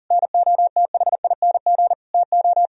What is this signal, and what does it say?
Morse code, heard as a keyed tone, about 45 wpm, 700 Hz. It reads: NOTHING TO